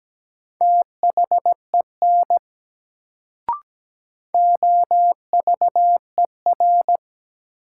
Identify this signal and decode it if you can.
Morse code, heard as a keyed tone, about 17 wpm, 700 Hz. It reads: THEN OVER
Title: Sets of 2 Words - New Word Drill - 17wpm